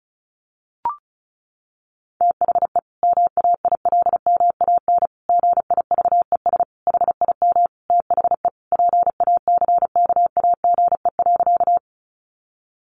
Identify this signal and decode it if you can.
Morse code, heard as a keyed tone, about 34 wpm, 700 Hz. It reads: THE MAILMAN GIVES HIM THE PACKAGE.
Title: Easy Phrases 35wpm